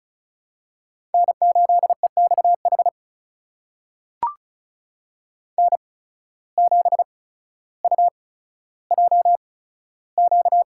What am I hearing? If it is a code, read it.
Morse code, heard as a keyed tone, about 36 wpm, spaced as for 15 wpm, 700 Hz. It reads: N8EXH N7UJQ
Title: USA Call Signs - 5 Characters - 8x Character Spacing - 35wpm